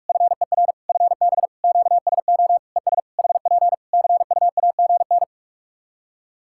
Morse code, heard as a keyed tone, about 41 wpm, 700 Hz. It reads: FER FB QSO ES HP CUAGN
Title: K1USN SST QSO Practice 45wpm